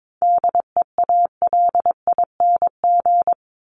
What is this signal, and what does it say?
Morse code, heard as a keyed tone, about 22 wpm, 700 Hz. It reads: DEALING
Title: ING Suffix - Top 100 Words with Base Words 22wpm